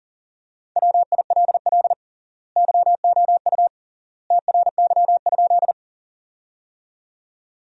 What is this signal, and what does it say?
Morse code, heard as a keyed tone, about 40 wpm, 700 Hz. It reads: WILL YOU TRY?
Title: 3x Word-Spacing - Sentences from Top 200 Words - 40wpm